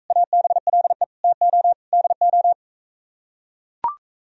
Morse code, heard as a keyed tone, about 44 wpm, 700 Hz. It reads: ABLE TO DO